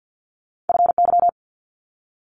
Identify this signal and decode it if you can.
Morse code, heard as a keyed tone, about 50 wpm, 690 Hz. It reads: FY
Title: Multi-letter Phonemes 50wpm